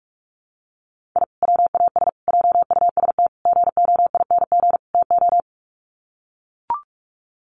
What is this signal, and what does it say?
Morse code, heard as a keyed tone, about 44 wpm, 700 Hz. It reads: I WAS JUST GOING TO
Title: Sets of 5 Words Encore 45wpm